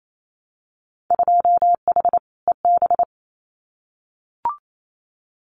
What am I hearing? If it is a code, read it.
Morse code, heard as a keyed tone, about 28 wpm, 700 Hz. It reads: 2H EB
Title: Winter Field Day Exchanges Encore 28wpm